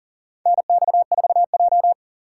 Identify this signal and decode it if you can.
Morse code, heard as a keyed tone, about 40 wpm, 700 Hz. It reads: NX4J
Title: USA Call Signs - 4 Characters - 6x Character Spacing - 40wpm